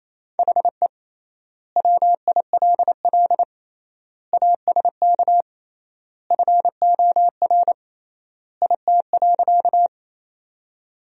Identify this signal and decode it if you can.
Morse code, heard as a keyed tone, about 27 wpm, 700 Hz. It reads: HE WILL ASK FOR IT.
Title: 3x Word-Spacing - Sentences from Top 200 Words - 28wpm